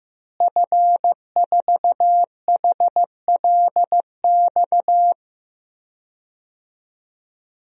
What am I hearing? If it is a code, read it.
Morse code, heard as a keyed tone, about 15 wpm, 700 Hz. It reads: F4HLX
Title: DXCC-100 Call Signs - 5 Characters - Repeated 2x - 15wpm